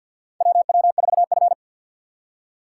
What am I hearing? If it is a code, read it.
Morse code, heard as a keyed tone, about 50 wpm, 700 Hz. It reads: WW4F
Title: USA Call Signs - 4 Characters - 3x Character Spacing - 50wpm